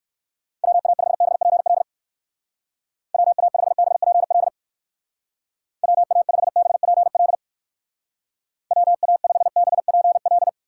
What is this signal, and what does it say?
Morse code, heard as a keyed tone, about 58 wpm, 700 Hz. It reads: WA5BPL WA5BPL WA5BPL WA5BPL